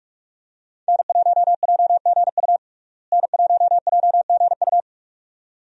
Morse code, heard as a keyed tone, about 46 wpm, 700 Hz. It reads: N1JGU N1JGU
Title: USA Call Signs - 5 Characters - Repeated 2x - 45wpm